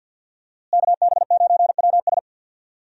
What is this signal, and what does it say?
Morse code, heard as a keyed tone, about 53 wpm, 700 Hz. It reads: KB9WS
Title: USA Call Signs - 5 Characters - Repeated 2x - 50wpm